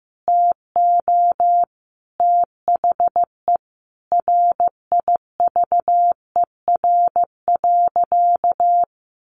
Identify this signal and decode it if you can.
Morse code, heard as a keyed tone, about 15 wpm, 700 Hz. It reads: TO THE RIVER.
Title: Sentences from Top 1500 Words 15wpm